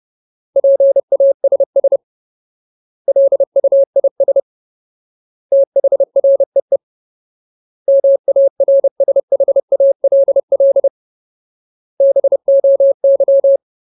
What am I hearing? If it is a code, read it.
Morse code, heard as a keyed tone, about 30 wpm, 550 Hz. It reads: PASS LUIS THREE MARSHALL BOY
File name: Ultimate Word List / 30WPM Ultimate